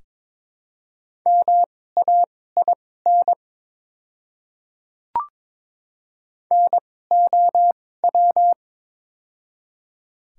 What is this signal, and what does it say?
Morse code, heard as a keyed tone, about 22 wpm, spaced as for 16 wpm, 700 Hz. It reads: MAIN NOW
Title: 2x Character Spacing - Top 300 Words - 22wpm